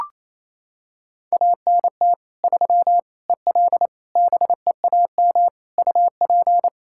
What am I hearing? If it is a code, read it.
Morse code, heard as a keyed tone, about 28 wpm, 700 Hz. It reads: ANT 3 EL BEAM UP
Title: Practice QSO - Exchange 2 - Info - 28wpm